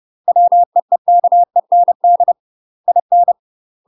Morse code, heard as a keyed tone, about 30 wpm, 700 Hz. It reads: WEEKEND IN
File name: UK MASTER QSOs TRAINING / 30WPM / UK MASTER QSOs 0051